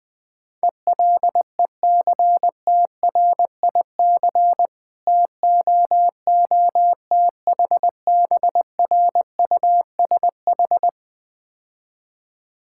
Morse code, heard as a keyed tone, about 20 wpm, 700 Hz. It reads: ELECTRIC TOOTHBRUSH
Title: Difficult Words - Speed-Racing with Hints - 20wpm